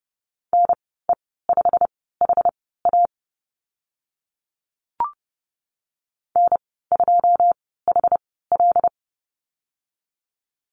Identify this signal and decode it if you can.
Morse code, heard as a keyed tone, about 30 wpm, spaced as for 17 wpm, 700 Hz. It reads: NE5HA N2HL
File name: Lesson 15g - Number 5 - Call Signs 30wpm